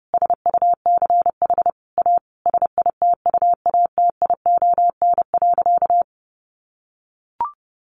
Morse code, heard as a keyed tone, about 30 wpm, 700 Hz. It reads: SUCH A SITUATION.
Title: Sentences from Top 1200 Words 30wpm